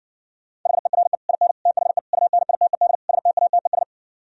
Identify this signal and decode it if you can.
Morse code, heard as a keyed tone, about 60 wpm, 700 Hz. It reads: HERE IN THE UNITED STATES